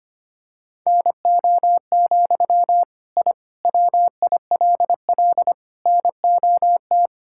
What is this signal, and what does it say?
Morse code, heard as a keyed tone, about 25 wpm, 700 Hz. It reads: NO, I WILL NOT
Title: Sentences from Top 1000 Words - Review Part 5 - 25wpm